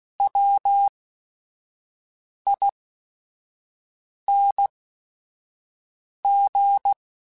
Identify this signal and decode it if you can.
Morse code, heard as a keyed tone, about 16 wpm, spaced as for 5 wpm, 800 Hz. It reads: WING
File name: Special INGs / Special INGs-mp3s / Sngl Ing @5